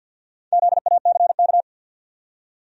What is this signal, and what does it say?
Morse code, heard as a keyed tone, about 51 wpm, 700 Hz. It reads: ZACK
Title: USA Common Names - Max 4 Letters - 50wpm